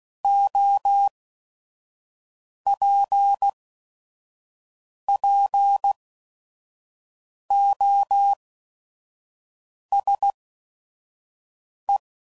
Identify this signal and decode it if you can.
Morse code, heard as a keyed tone, about 16 wpm, 800 Hz. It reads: O P P O S E